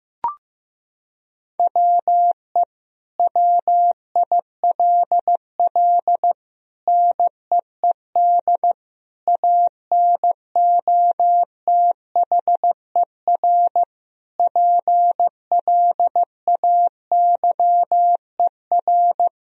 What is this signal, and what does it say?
Morse code, heard as a keyed tone, about 15 wpm, 700 Hz. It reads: WE WILL NEED ANOTHER PLAYER
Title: Sentences from Top 1000 Words - Review Part 4 - 15wpm